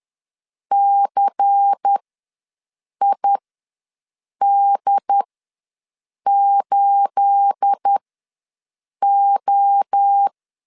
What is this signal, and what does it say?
Morse code, heard as a keyed tone, about 10 wpm, spaced as for 6 wpm, 800 Hz. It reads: CID8O
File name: LEZIONI / 9 28 ws 10wpm 4124